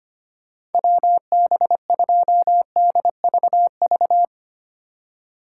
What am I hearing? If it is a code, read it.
Morse code, heard as a keyed tone, about 31 wpm, 700 Hz. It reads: WB2DVV
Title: Speed Racing - USA 2x3 Call Signs 25wpm